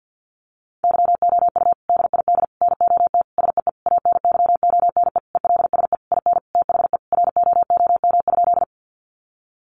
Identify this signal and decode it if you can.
Morse code, heard as a keyed tone, about 50 wpm, 700 Hz. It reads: YOU DID NOT SEE ANYONE ELSE IN THE ROOM?
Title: Sentences from Top 900 Words 50wpm